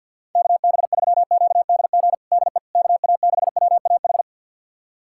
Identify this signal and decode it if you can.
Morse code, heard as a keyed tone, about 52 wpm, 700 Hz. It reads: KB3QDG DE KA6WAH
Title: USA Callsign DE Callsign - 6 Characters - 50wpm